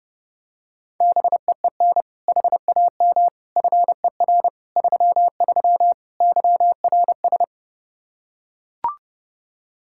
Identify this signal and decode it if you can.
Morse code, heard as a keyed tone, about 30 wpm, 700 Hz. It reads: BEEN HAM FER 33 YRS
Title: Practice QSO - Exchange 3 - More Info - 30wpm